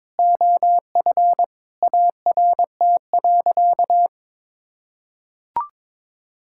Morse code, heard as a keyed tone, about 22 wpm, 700 Hz. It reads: OF ART.
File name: practice / 22wpm / Sentences with 4-Word Phrases - 22wpm